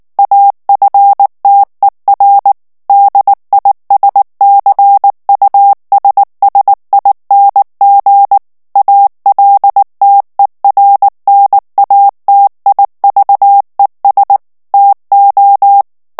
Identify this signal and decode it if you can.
Morse code, heard as a keyed tone, about 19 wpm, 800 Hz. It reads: AFTER DISCUSSING ALTERNATIVES TO
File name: ANGLAIS texte clair 3 vitesse 18 mots minute